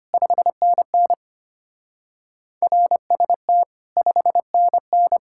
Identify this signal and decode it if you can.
Morse code, heard as a keyed tone, about 31 wpm, 700 Hz. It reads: 5NN RST 5NN
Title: Speed-Racing Common QSO Phrases - 25wpm